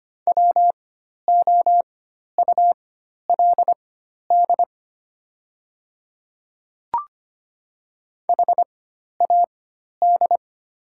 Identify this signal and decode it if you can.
Morse code, heard as a keyed tone, about 25 wpm, spaced as for 12 wpm, 700 Hz. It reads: WOULD HAD
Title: 4x Character Spacing - Top 100 Words - 25wpm